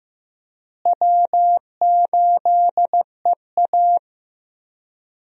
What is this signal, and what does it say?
Morse code, heard as a keyed tone, about 16 wpm, 700 Hz. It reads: W8EA
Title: USA Call Signs - 4 Characters - Repeated 3x - 15wpm